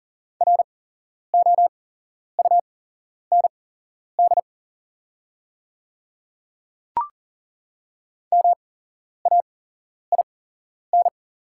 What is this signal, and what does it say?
Morse code, heard as a keyed tone, about 40 wpm, spaced as for 11 wpm, 690 Hz. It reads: ROUND MAIN
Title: 8x Character Spacing - Top 300 Words - 40wpm